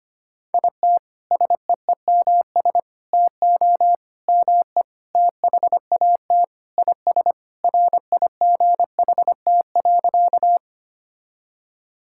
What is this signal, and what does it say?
Morse code, heard as a keyed tone, about 25 wpm, 700 Hz. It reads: IT SEEMS TO ME THAT IS RIGHT.
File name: practice / 25wpm / Sentences with 4-Word Phrases - 25wpm